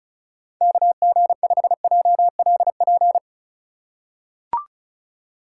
Morse code, heard as a keyed tone, about 36 wpm, 700 Hz. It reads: KG5JLP
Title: USA Call Signs - 6 Characters - 2x Character Spacing - 35wpm